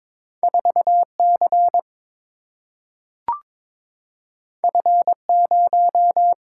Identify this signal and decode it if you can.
Morse code, heard as a keyed tone, about 22 wpm, 700 Hz. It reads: 4 C F 0
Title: Single-Letter Single-Number 22wpm